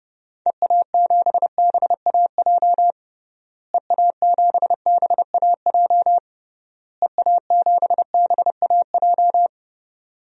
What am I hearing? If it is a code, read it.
Morse code, heard as a keyed tone, about 30 wpm, 700 Hz. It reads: EA7BAJ EA7BAJ EA7BAJ
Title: DXCC-100 Call Signs - 6 Characters - Repeated 3x - 30wpm